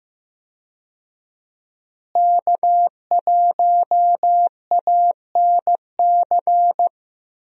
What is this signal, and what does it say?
Morse code, heard as a keyed tone, about 16 wpm, 700 Hz. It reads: K1ANC
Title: USA Call Signs - 5 Characters - Repeated 2x - 15wpm